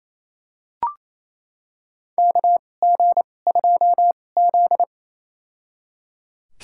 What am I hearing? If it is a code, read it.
Morse code, heard as a keyed tone, about 28 wpm, spaced as for 20 wpm, 700 Hz. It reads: KG2Z
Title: USA Call Signs - 2x Character Spacing - 28wpm